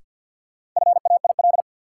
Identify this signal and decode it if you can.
Morse code, heard as a keyed tone, about 52 wpm, 700 Hz. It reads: FAIL
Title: Sentences from Top 1100 Words - New Word Drill Alt - 50wpm